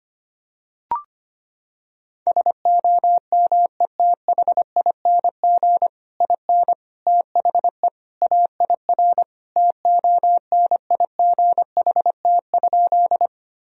Morse code, heard as a keyed tone, about 25 wpm, 700 Hz. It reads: SOMETHING IN THE AIR TONIGHT?
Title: Sentences from Top 1000 Words - Review Part 1 - 25wpm